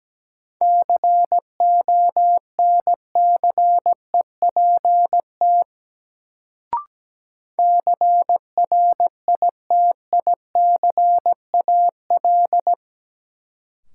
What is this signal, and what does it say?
Morse code, heard as a keyed tone, about 17 wpm, 700 Hz. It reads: CONCEPT CRITICAL
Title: Sentences from Top 1100 Words - New Word Drill Alt - 17wpm